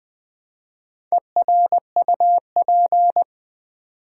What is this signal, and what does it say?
Morse code, heard as a keyed tone, about 21 wpm, 700 Hz. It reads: ERUP
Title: Build A Word - 1 Letter At A Time - 3 to 5-Letter Words - 20wpm